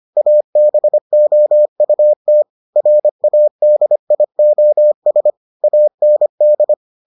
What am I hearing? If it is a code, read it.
Morse code, heard as a keyed tone, about 25 wpm, 600 Hz. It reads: ABOUT RADIOS AND